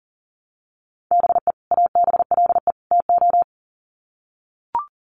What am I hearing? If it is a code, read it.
Morse code, heard as a keyed tone, about 40 wpm, 700 Hz. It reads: BE ABLE TO
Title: Sets of 3 Words 40wpm